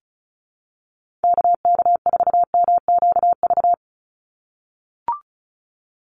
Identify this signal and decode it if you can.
Morse code, heard as a keyed tone, about 34 wpm, 700 Hz. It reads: KK4MQV
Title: USA Call Signs 35wpm